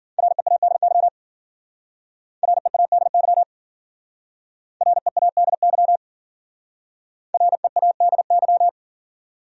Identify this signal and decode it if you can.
Morse code, heard as a keyed tone, about 47 wpm, 700 Hz. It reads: READY READY READY READY
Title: Speed-Racing Top 500 Words 40wpm